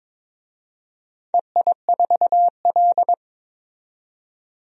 Morse code, heard as a keyed tone, about 22 wpm, 700 Hz. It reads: EI4L
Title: DXCC-100 Call Signs - 4 Characters - 22wpm